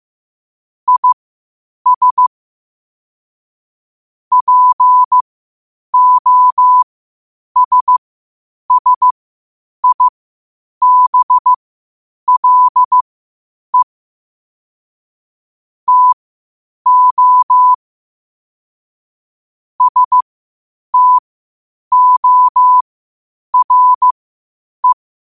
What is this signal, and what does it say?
Morse code, heard as a keyed tone, about 15 wpm, spaced as for 8 wpm, 1000 Hz. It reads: IS POSSIBLE TO STORE
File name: qotdfiles / 8wpm